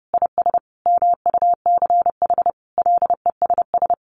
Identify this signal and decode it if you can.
Morse code, heard as a keyed tone, about 30 wpm, 700 Hz. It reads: IS MUCH LESS
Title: Sentences from Top 1300 Words 30wpm